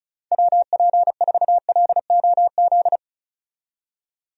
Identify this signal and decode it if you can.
Morse code, heard as a keyed tone, about 34 wpm, 700 Hz. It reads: WP4LOZ
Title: USA Call Signs - 6 Characters - 35wpm